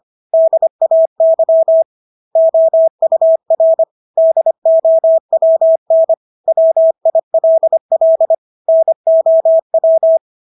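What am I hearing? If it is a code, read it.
Morse code, heard as a keyed tone, about 25 wpm, 650 Hz. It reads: DAY OUR DOWN WILL NOW
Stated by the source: Morse practice of English words at 25 wpm from CW med Gnister.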